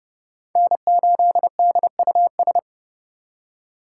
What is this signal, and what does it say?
Morse code, heard as a keyed tone, about 31 wpm, 700 Hz. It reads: N8DUS
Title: K1USN SST QSO Practice - 30wpm